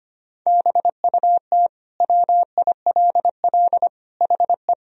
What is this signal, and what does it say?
Morse code, heard as a keyed tone, about 25 wpm, 700 Hz. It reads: BUT WILL HE
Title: Sentences from Top 1000 Words - Review Part 3 - 25wpm